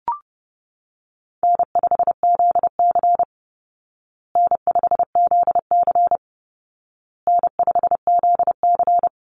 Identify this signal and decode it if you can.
Morse code, heard as a keyed tone, about 30 wpm, 700 Hz. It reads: N5ZC N5ZC N5ZC
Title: Easy Call Signs Repeated 3x - 30wpm